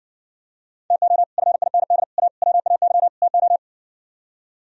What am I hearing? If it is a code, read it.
Morse code, heard as a keyed tone, about 60 wpm, 700 Hz. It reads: TO FIND A WAY TO